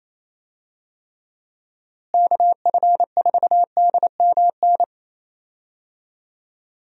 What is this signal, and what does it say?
Morse code, heard as a keyed tone, about 28 wpm, 700 Hz. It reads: KF4DMN
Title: USA Call Signs - 6 Characters - Repeated 3x - 28wpm